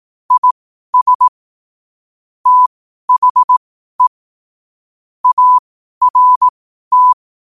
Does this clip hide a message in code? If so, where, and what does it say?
Yes: Morse code from 0.3 to 7.1 s, heard as a keyed tone, about 18 wpm, spaced as for 12 wpm, 1000 Hz. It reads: IS THE ART